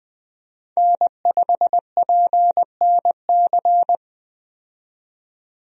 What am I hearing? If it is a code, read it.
Morse code, heard as a keyed tone, about 20 wpm, 700 Hz. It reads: N5PNC
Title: USA Call Signs - 5 Characters - 20wpm